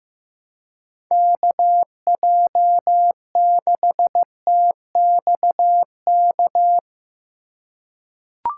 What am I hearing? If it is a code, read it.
Morse code, heard as a keyed tone, about 16 wpm, 700 Hz. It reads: KJ6TXK
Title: USA Call Signs - 6 Characters - 3x Character Spacing - 15wpm